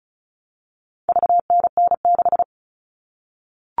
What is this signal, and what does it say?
Morse code, heard as a keyed tone, about 34 wpm, 700 Hz. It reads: VNN6
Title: All Cut Numbers - 4-Digit 35wpm